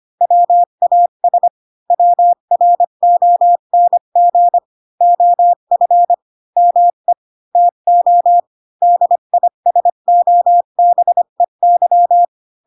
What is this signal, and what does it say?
Morse code, heard as a keyed tone, about 25 wpm, 700 Hz. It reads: WAS WRONG OF ME TO DISOBEY
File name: Pinocchio / Pinocchio ch5 20 25 ik1hge